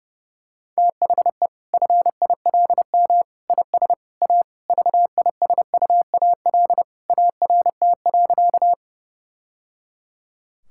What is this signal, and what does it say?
Morse code, heard as a keyed tone, about 32 wpm, 700 Hz. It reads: THE FILM IS A VISUAL ART.
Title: Sentences from Top 1000 Words - Review Part 6 - 30wpm